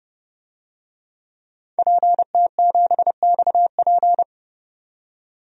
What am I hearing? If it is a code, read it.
Morse code, heard as a keyed tone, about 30 wpm, 690 Hz. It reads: PT7XP
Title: DXCC-100 Call Signs - 5 Characters - 30wpm